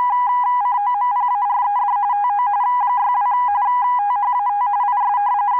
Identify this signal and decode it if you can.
RTTY, about 45.45 baud. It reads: COMPARED TO 62 THAT IS